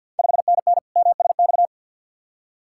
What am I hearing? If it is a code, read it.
Morse code, heard as a keyed tone, about 45 wpm, 690 Hz. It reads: 5NN MSX
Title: Ontario QSO Party Exchange 50wpm